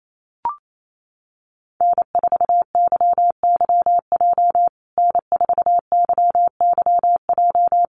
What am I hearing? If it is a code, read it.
Morse code, heard as a keyed tone, about 28 wpm, 700 Hz. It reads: N4YYJ N4YYJ
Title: USA Call Signs Repeated 3x 28wpm